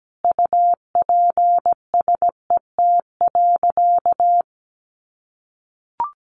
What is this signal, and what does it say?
Morse code, heard as a keyed tone, about 17 wpm, 700 Hz. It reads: UPSET.
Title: Sentences from Top 800 Words 17wpm